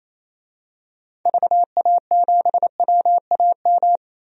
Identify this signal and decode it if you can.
Morse code, heard as a keyed tone, about 28 wpm, 700 Hz. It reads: VA7WAM
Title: DXCC-100 Call Signs - 6 Characters - 28wpm